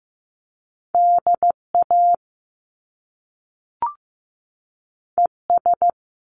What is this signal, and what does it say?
Morse code, heard as a keyed tone, about 15 wpm, 700 Hz. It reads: DA ES
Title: Top Two Letter Combinations in Words 15wpm